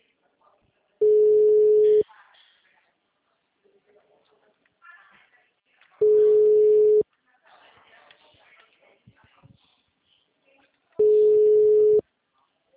Ringback tone (Europe, 425 Hz).